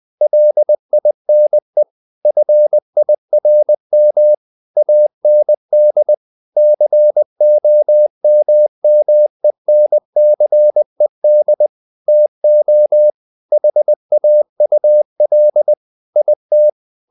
Morse code, heard as a keyed tone, about 20 wpm, 600 Hz. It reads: LINE FIRM AND COMMENCED TO HAUL IT
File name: the old man and the sea / the old man and the sea part 1 20wpm